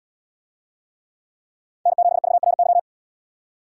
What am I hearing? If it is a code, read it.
Morse code, heard as a keyed tone, about 70 wpm, 700 Hz. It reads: N7VRY